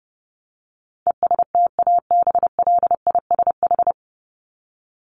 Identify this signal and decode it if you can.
Morse code, heard as a keyed tone, about 28 wpm, 700 Hz. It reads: ESTABLISH
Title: ED Suffix - Top 100 Words with Base Words 30wpm